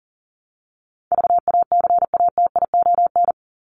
Morse code, heard as a keyed tone, about 40 wpm, 700 Hz. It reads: VACATION
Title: Top 1501-1750 Words 40wpm